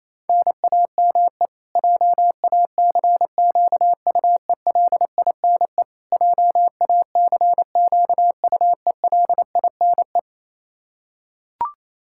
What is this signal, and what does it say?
Morse code, heard as a keyed tone, about 27 wpm, 700 Hz. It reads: NAME JACQUELINE JACQUELINE
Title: Practice QSO - Exchange 1 - Beginning - 28wpm